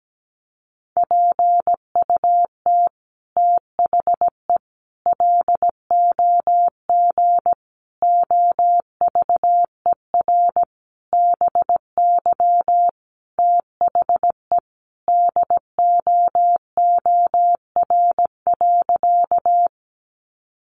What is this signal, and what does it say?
Morse code, heard as a keyed tone, about 17 wpm, 700 Hz. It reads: PUT THE LOG OVER BY THE DOOR.
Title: Sentences from Top 1400 Words 17wpm